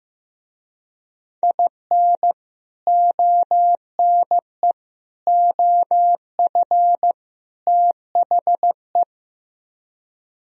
Morse code, heard as a keyed tone, about 17 wpm, 700 Hz. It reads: IN ONE OF THE
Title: Speed-Racing - Sets of 4 Words 15wpm